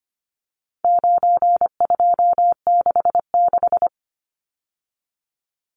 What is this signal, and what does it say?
Morse code, heard as a keyed tone, about 25 wpm, 700 Hz. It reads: 9266
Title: Numbers - 4-Digit 25wpm